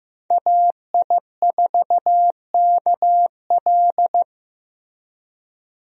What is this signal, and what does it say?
Morse code, heard as a keyed tone, about 15 wpm, 700 Hz. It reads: AI4KL
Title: USA Call Signs - 5 Characters - Repeated 2x - 15wpm